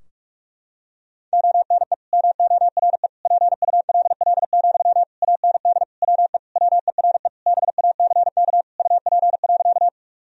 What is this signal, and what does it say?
Morse code, heard as a keyed tone, about 46 wpm, 700 Hz. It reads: ONE MORE PULL, AND WE WERE BACK UP.
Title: Sentences from Top 1000 Words - Review Part 2 - 45wpm